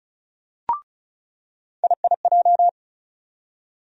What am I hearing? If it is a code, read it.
Morse code, heard as a keyed tone, about 36 wpm, 700 Hz. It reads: IIJ
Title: USA Call Sign Postfixes - 3 Letters 35wpm